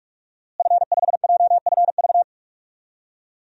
Morse code, heard as a keyed tone, about 41 wpm, 700 Hz. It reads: F5JFV
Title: DXCC-100 Call Signs - 5 Characters - Repeated 2x - 45wpm